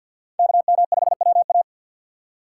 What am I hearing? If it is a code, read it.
Morse code, heard as a keyed tone, about 47 wpm, 700 Hz. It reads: KM5WA
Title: USA Call Signs - 5 Characters - 2x Character Spacing - 50wpm